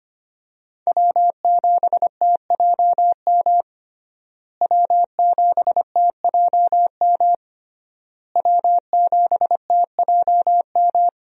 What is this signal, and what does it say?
Morse code, heard as a keyed tone, about 25 wpm, 700 Hz. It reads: W7TJM W7TJM W7TJM
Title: USA Call Signs - 5 Characters - Repeated 3x - 25wpm